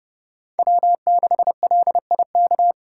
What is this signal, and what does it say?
Morse code, heard as a keyed tone, about 31 wpm, 700 Hz. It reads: W6LIK
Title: USA Call Signs - 4x Character Spacing - 30wpm